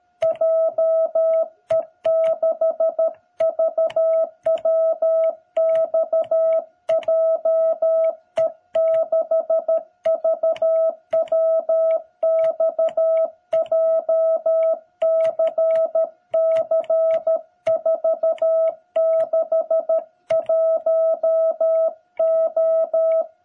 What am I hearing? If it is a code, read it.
Morse code, heard as a keyed tone, about 13 wpm, 660 Hz. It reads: JE6VWX JE6VWX JCC461O
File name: jh8cbh / 2021cw